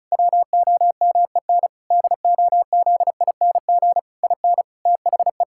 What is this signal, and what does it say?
Morse code, heard as a keyed tone, about 36 wpm, 700 Hz. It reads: WOMEN DOZING IN THE
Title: Pangram Challenge v2 - 35wpm